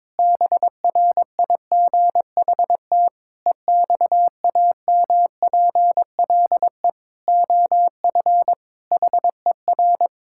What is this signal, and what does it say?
Morse code, heard as a keyed tone, about 22 wpm, 700 Hz. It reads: BRIGHT EXAMPLE OF HER
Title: Sentences from Top 1600 Words 22wpm